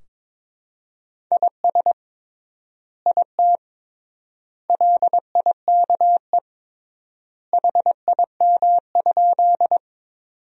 Morse code, heard as a keyed tone, about 22 wpm, 700 Hz. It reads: IS IT LIKE HIM?
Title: 3x Word-Spacing - Sentences from Top 100 Words - 22wpm